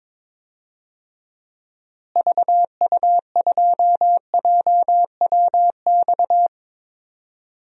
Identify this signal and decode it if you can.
Morse code, heard as a keyed tone, about 22 wpm, 700 Hz. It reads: VU2JWX
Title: DXCC-100 Call Signs - 6 Characters - Repeated 2x - 22wpm